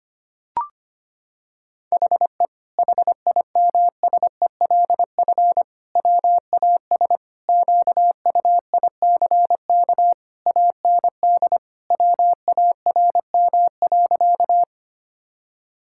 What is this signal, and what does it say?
Morse code, heard as a keyed tone, about 25 wpm, 700 Hz. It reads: HE HIMSELF WAS QUICK AND WARM.
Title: Sentences from Top 1600 Words 25wpm